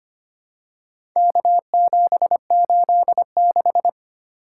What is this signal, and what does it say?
Morse code, heard as a keyed tone, about 25 wpm, 700 Hz. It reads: K786
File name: practice / 25wpm / Four Letter-Number Combinations - 25wpm